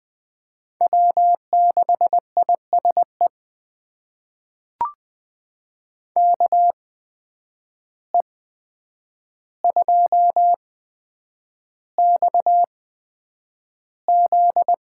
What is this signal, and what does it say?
Morse code, heard as a keyed tone, about 20 wpm, spaced as for 9 wpm, 700 Hz. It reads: W6ISE KE2XZ
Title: USA Call Signs - 5 Characters - 8x Character Spacing - 20wpm